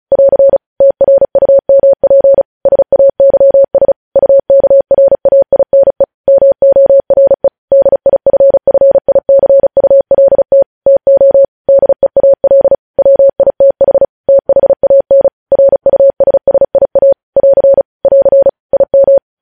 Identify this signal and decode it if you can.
Morse code, heard as a keyed tone, about 36 wpm, 550 Hz. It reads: <AR> TRUMP SAYS UKRAINE MORE DIFFICULT TO DEAL WITH THAN RUSSIA <AR> <AR> IM
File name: podcast / News headlines-1x-35x35